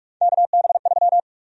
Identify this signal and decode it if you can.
Morse code, heard as a keyed tone, about 42 wpm, 700 Hz. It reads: KB3
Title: USA Call Sign Prefixes - Repeated 2x - 45wpm